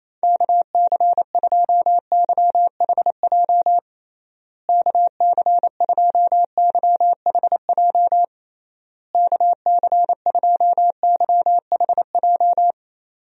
Morse code, heard as a keyed tone, about 28 wpm, 700 Hz. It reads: KC2YHJ KC2YHJ KC2YHJ
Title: USA Call Signs - 6 Characters - Repeated 3x - 28wpm